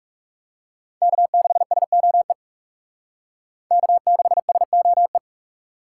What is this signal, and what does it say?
Morse code, heard as a keyed tone, about 38 wpm, 700 Hz. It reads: K6SOE K6SOE